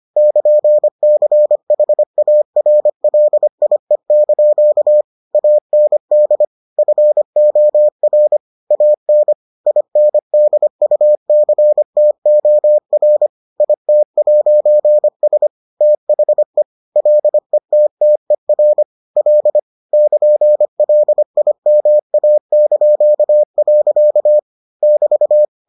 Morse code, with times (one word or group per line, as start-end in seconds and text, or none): 0.2-5.0 <KN>CHARLIE)
5.3-6.4 AND
6.8-8.4 FOR
8.7-9.3 AN
9.7-13.3 INDUCTOR
13.6-15.5 IT'S
15.8-16.6 THE
17.0-18.8 LETTER
19.2-19.6 L
19.9-24.4 <KN>LIMA).
24.8-25.4 <BT>